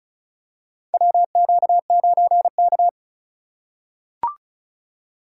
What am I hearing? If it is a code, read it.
Morse code, heard as a keyed tone, about 36 wpm, 700 Hz. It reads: WQ9K
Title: USA Call Signs - 4 Characters - 3x Character Spacing - 35wpm